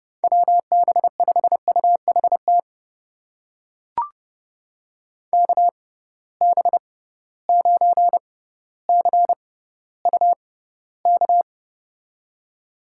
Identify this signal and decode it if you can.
Morse code, heard as a keyed tone, about 32 wpm, spaced as for 16 wpm, 700 Hz. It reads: WB5UHT KB9CUK
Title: USA Call Signs - 6 Characters - 6x Character Spacing - 30wpm